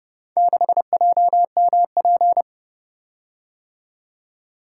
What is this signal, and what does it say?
Morse code, heard as a keyed tone, about 31 wpm, 700 Hz. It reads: 6JMP